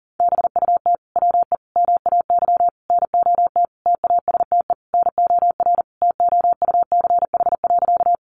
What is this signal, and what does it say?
Morse code, heard as a keyed tone, about 40 wpm, 700 Hz. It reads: BUT WE MAY NOT TASTE NOR TOUCH.
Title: Sentences from Top 1500 Words 40wpm